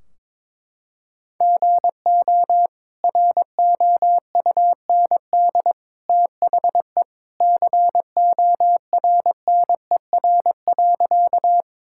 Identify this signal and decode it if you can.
Morse code, heard as a keyed tone, about 22 wpm, 700 Hz. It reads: GO ROUND THE CORNER.
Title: Sentences from Top 1600 Words 22wpm